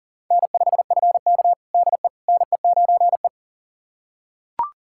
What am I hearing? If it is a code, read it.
Morse code, heard as a keyed tone, about 37 wpm, 700 Hz. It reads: N5FK DE NE9E
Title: USA Callsign DE Callsign - 4 Characters - 40wpm